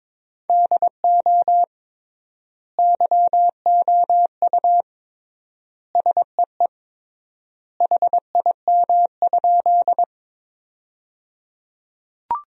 Morse code, heard as a keyed tone, about 22 wpm, 700 Hz. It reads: DO YOU SEE HIM?
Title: 3x Word-Spacing - Sentences from Top 100 Words - 22wpm